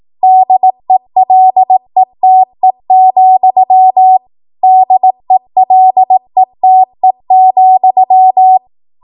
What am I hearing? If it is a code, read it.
Morse code, heard as a keyed tone, about 18 wpm, 750 Hz. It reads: DELETE, DELETE,
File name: Morse / 18WPM